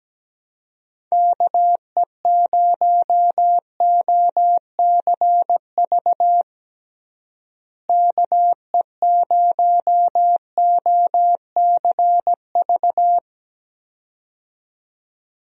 Morse code, heard as a keyed tone, about 17 wpm, 700 Hz. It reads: KE0OCV KE0OCV
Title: USA Call Signs - 6 Characters - Repeated 2x - 17wpm